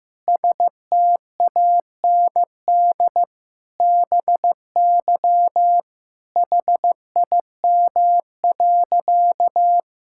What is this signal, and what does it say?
Morse code, heard as a keyed tone, about 15 wpm, 700 Hz. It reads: STAND BY HIM.
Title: Sentences from Top 1000 Words - Review Part 5 - 15wpm